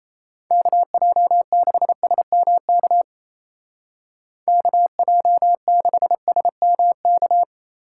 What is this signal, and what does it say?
Morse code, heard as a keyed tone, about 34 wpm, 700 Hz. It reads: KJ6SMK KJ6SMK
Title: Speed Racing - USA 2x3 Call Signs 28wpm